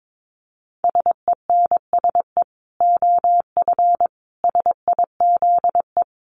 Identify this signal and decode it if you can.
Morse code, heard as a keyed tone, about 22 wpm, 700 Hz. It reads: SENSE OF SIZE
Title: Dit Challenge 22wpm